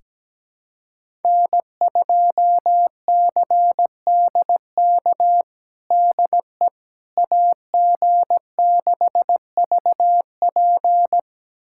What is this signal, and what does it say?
Morse code, heard as a keyed tone, about 17 wpm, 700 Hz. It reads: N2CDK DE AG6VP
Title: USA Callsign DE Callsign - 5 Characters - 17wpm